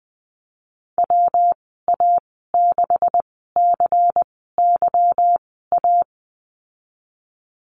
Morse code, heard as a keyed tone, about 19 wpm, spaced as for 14 wpm, 700 Hz. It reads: WA6CYA
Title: 161 Lesson 26g - Letter g - Call Signs 20wpm